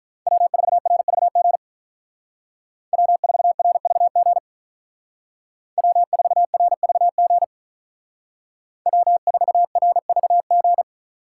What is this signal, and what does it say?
Morse code, heard as a keyed tone, about 46 wpm, 700 Hz. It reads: W4RVG W4RVG W4RVG W4RVG